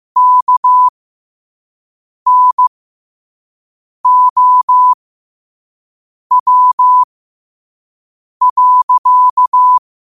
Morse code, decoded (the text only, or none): KNOW.